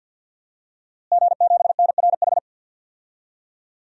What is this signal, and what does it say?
Morse code, heard as a keyed tone, about 49 wpm, 700 Hz. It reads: G7NRH
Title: DXCC-100 Call Signs - 5 Characters - Repeated 3x - 50wpm